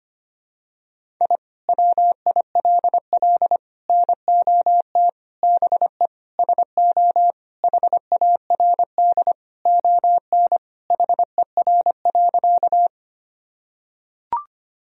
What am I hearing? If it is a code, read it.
Morse code, heard as a keyed tone, about 25 wpm, 700 Hz. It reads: I WILL NOT BE SO HARD ON HER.
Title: Sentences from Top 1000 Words - Review Part 1 - 25wpm